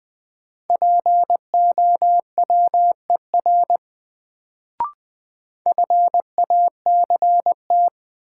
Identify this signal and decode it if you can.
Morse code, heard as a keyed tone, about 20 wpm, 700 Hz. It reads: POWER FACT
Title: Sentences from Top 500 Words - New Word Drill - 20wpm